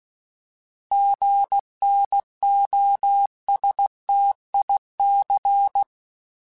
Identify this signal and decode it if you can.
Morse code, heard as a keyed tone, about 16 wpm, 800 Hz. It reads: GNOSTIC
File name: Alphabetical / Alphabetical mp3s / Gs @15